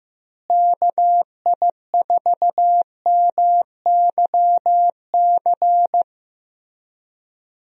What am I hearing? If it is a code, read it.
Morse code, heard as a keyed tone, about 15 wpm, 700 Hz. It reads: KI4MYC
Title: USA Call Signs - 6 Characters - Repeated 3x - 15wpm